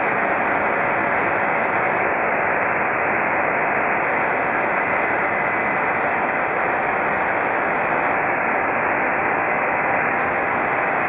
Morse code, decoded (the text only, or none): none